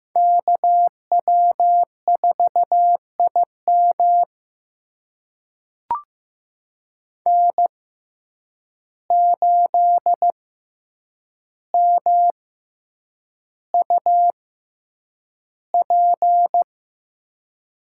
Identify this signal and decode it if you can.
Morse code, heard as a keyed tone, about 15 wpm, spaced as for 8 wpm, 700 Hz. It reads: KW4IM N8MUP
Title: USA Call Signs - 5 Characters - 6x Character Spacing - 15wpm